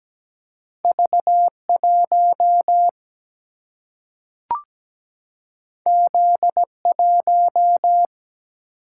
Morse code, heard as a keyed tone, about 17 wpm, 700 Hz. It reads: V1 Z1
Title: Single-Letter Single-Number 17wpm